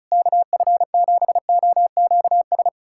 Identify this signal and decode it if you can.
Morse code, heard as a keyed tone, about 36 wpm, 700 Hz. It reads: KF7OQS
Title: USA Callsign DE Callsign 35wpm